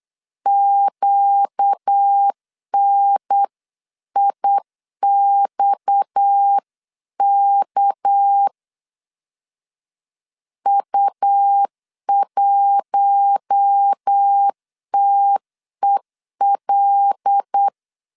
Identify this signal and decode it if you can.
Morse code, heard as a keyed tone, about 8 wpm, 800 Hz. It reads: QNIXK U1TEL
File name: 8wpm